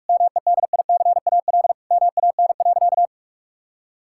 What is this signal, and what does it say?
Morse code, heard as a keyed tone, about 46 wpm, 700 Hz. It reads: MEDICAL MAN.
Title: Sentences from Top 1000 Words - Review Part 5 - 45wpm